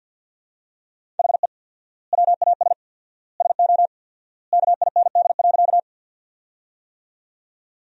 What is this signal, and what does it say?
Morse code, heard as a keyed tone, about 50 wpm, 700 Hz. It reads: HE WAS SO KIND.
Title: 4x Word-Spacing - Sentences from Top 200 Words - 50wpm